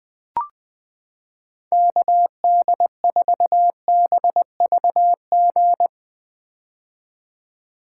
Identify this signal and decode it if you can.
Morse code, heard as a keyed tone, about 20 wpm, 700 Hz. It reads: KD4BVG
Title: 2023 POTA Activator Call Signs - 20wpm